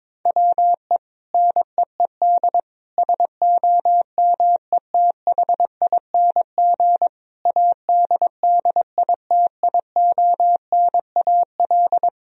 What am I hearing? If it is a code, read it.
Morse code, heard as a keyed tone, about 21 wpm, 700 Hz. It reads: WE NEED SOMETHING ADDITIONAL
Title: Sentences from Top 1000 Words - Review Part 4 - 22wpm